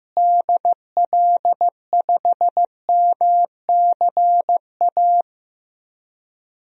Morse code, heard as a keyed tone, about 15 wpm, 700 Hz. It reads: DL5MCA
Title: DXCC-100 Call Signs - 6 Characters - Repeated 2x - 15wpm